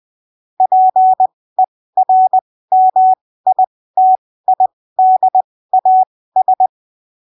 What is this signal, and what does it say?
Morse code, heard as a keyed tone, about 20 wpm, spaced as for 15 wpm, 750 Hz. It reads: PERMITIDAS
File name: arquivos / texto / texto 26 20ppm